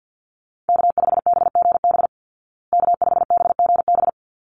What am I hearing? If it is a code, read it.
Morse code, heard as a keyed tone, about 50 wpm, 700 Hz. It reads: K5BGB K5BGB
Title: Call Signs Repeated 3x - 50wpm